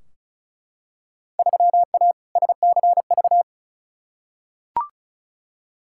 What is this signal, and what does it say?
Morse code, heard as a keyed tone, about 36 wpm, 700 Hz. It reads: 3A SCV
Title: Field Day Exchanges - 35wpm